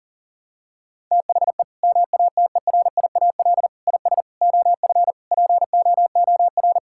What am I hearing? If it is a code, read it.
Morse code, heard as a keyed tone, about 40 wpm, 700 Hz. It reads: THE MATERIAL IS OF POOR
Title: Sentences from Top 1000 Words - Review Part 5 - 40wpm